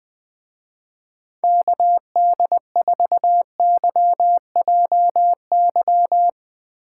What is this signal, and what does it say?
Morse code, heard as a keyed tone, about 22 wpm, 700 Hz. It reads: KD4YJY
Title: Speed Racing - USA Call Signs 20wpm